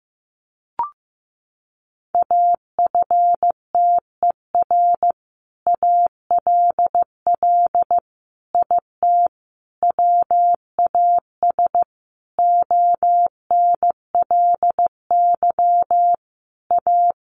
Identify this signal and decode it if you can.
Morse code, heard as a keyed tone, about 15 wpm, 700 Hz. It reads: AFTER ALL IT WAS ONLY A
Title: Sentences from Top 1200 Words 15wpm